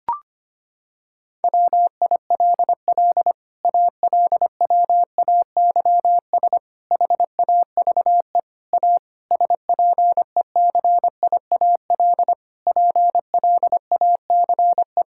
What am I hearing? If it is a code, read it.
Morse code, heard as a keyed tone, about 25 wpm, 700 Hz. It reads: WILL ALWAYS HAVE A SPECIAL PLACE
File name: Sets of 9 Words - 25wpm